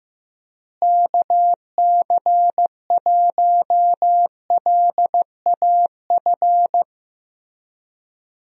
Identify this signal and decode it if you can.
Morse code, heard as a keyed tone, about 15 wpm, 700 Hz. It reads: KC1LAF
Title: USA Call Signs - 6 Characters - Repeated 3x - 15wpm